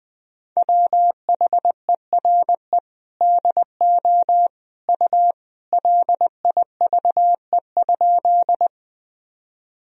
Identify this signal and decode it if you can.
Morse code, heard as a keyed tone, about 20 wpm, 700 Hz. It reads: WHERE DO U LIVE?
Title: Sentences from Top 200 Words and Abbreviations 20wpm